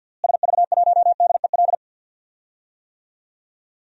Morse code, heard as a keyed tone, about 47 wpm, 700 Hz. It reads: SV1DEL
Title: DXCC-100 Call Signs - 6 Characters - Repeated 2x - 50wpm